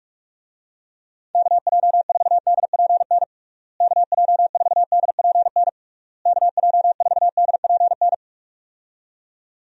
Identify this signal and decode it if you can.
Morse code, heard as a keyed tone, about 46 wpm, 700 Hz. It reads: KJ4DPN KJ4DPN KJ4DPN
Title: USA Call Signs - 6 Characters - Repeated 3x - 45wpm